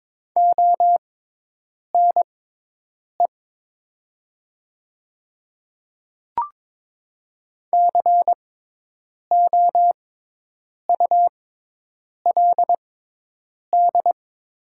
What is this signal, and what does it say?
Morse code, heard as a keyed tone, about 22 wpm, spaced as for 8 wpm, 700 Hz. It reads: ONE COULD
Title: 6x Character Spacing - Top 100 Words - 22wpm